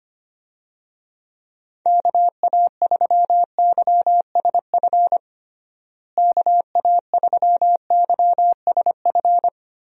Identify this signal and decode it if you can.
Morse code, heard as a keyed tone, about 25 wpm, 700 Hz. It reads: KA3YSF KA3YSF
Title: USA Call Signs - 6 Characters - Repeated 3x - 25wpm